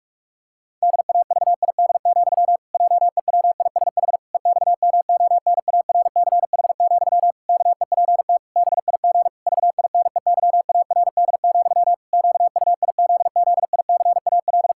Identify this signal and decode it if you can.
Morse code, heard as a keyed tone, about 48 wpm, 700 Hz. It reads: DAVID, JEWISH EXMONARCH, KEPT BIG VINEYARD, QUIZZICAL
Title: Pangram Challenge v2 - 45wpm